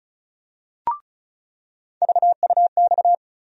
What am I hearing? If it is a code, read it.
Morse code, heard as a keyed tone, about 36 wpm, 700 Hz. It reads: VUX
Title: USA Call Sign Postfixes - 3 Letters 35wpm